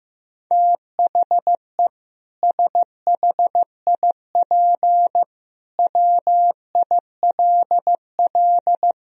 Morse code, heard as a keyed tone, about 15 wpm, 700 Hz. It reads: THE SHIP WILL
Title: Sentences from Top 1000 Words - Review Part 4 - 15wpm